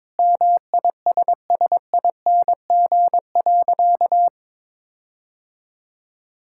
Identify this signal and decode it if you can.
Morse code, heard as a keyed tone, about 22 wpm, 700 Hz. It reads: MISSING.
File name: practice / 22wpm / Sentences with 4-Word Phrases - 22wpm